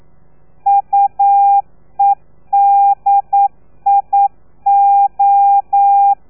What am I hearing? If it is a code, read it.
Morse code, heard as a keyed tone, about 9 wpm, 800 Hz. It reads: UEDIO
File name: telegrafia / i2tfj / Lez3 40